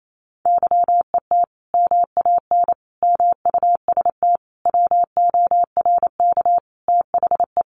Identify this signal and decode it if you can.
Morse code, heard as a keyed tone, about 28 wpm, 700 Hz. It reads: YET MAN MUST WORK THE
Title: Sentences from Top 1500 Words 28wpm